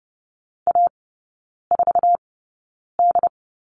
Morse code, heard as a keyed tone, about 30 wpm, spaced as for 9 wpm, 700 Hz. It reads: A4D
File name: 081 Lesson 14e - Number 2 - Three Characters 30wpm